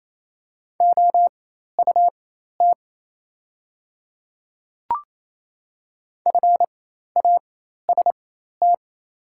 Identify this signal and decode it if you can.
Morse code, heard as a keyed tone, about 28 wpm, spaced as for 13 wpm, 700 Hz. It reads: OUT FAST
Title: 4x Character Spacing - Top 500 Words - 28wpm